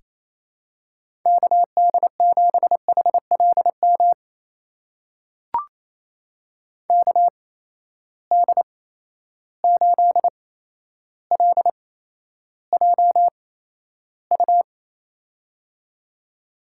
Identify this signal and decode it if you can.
Morse code, heard as a keyed tone, about 27 wpm, spaced as for 12 wpm, 700 Hz. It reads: KD7HLM KD8LJU
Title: USA Call Signs - 8x Character Spacing - 28wpm